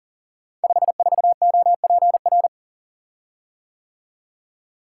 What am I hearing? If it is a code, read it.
Morse code, heard as a keyed tone, about 42 wpm, 700 Hz. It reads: 54OPR
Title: Code Groups - 5 Letter:Numbers - 40wpm